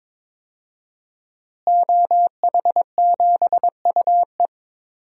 Morse code, heard as a keyed tone, about 22 wpm, 700 Hz. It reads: OH7UE
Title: DXCC-100 Call Signs - 5 Characters - Repeated 3x - 22wpm